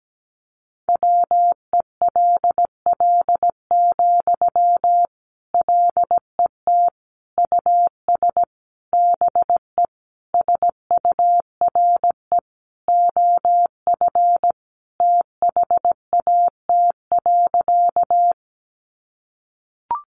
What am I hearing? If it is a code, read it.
Morse code, heard as a keyed tone, about 17 wpm, 700 Hz. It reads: WELL, LET US BE SURE OF THAT.
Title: Sentences from Top 300 Words 17wpm